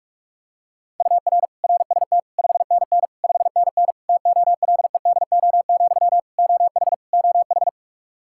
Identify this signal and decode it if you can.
Morse code, heard as a keyed tone, about 46 wpm, 700 Hz. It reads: UR RST 5NN 5NN TOLEDO, OH OH
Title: Practice QSO - Exchange 1 - Beginning - 45wpm